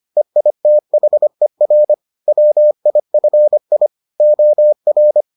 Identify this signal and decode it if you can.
Morse code, heard as a keyed tone, about 25 wpm, 600 Hz. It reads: EITHER WIFI OR